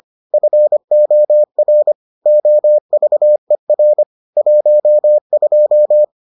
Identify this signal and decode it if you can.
Morse code, heard as a keyed tone, about 25 wpm, 600 Hz. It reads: FOR OVER 12